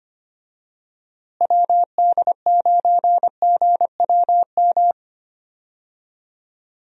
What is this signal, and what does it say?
Morse code, heard as a keyed tone, about 25 wpm, 700 Hz. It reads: WD9GWM
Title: USA Call Signs - 6 Characters - Repeated 2x - 25wpm